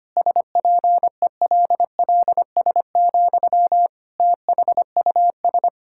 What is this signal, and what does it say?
Morse code, heard as a keyed tone, about 25 wpm, 700 Hz. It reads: SPELLS, THUS
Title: Pangram Challenge v2 - 25wpm